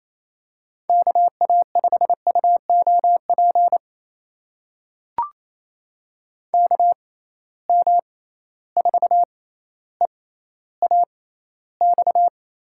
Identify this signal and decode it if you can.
Morse code, heard as a keyed tone, about 28 wpm, spaced as for 14 wpm, 700 Hz. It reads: KA5UOP KM4EAX
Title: USA Call Signs - 6 Characters - 6x Character Spacing - 28wpm